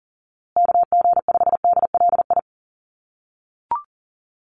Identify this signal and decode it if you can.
Morse code, heard as a keyed tone, about 40 wpm, 700 Hz. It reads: KG5DLI
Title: USA Call Signs Repeated 3x 40wpm